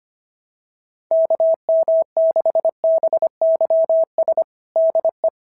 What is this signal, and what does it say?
Morse code, heard as a keyed tone, about 25 wpm, 690 Hz. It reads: KM6BYS DE
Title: SOTA QSO Practice 25wpm